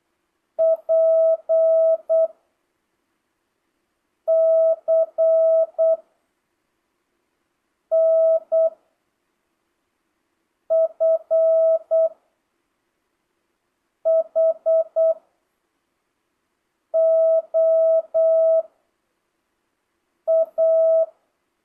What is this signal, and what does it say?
Morse code, heard as a keyed tone, about 8 wpm, 650 Hz. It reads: P C N F H O A
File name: jh8cbh / 2025cw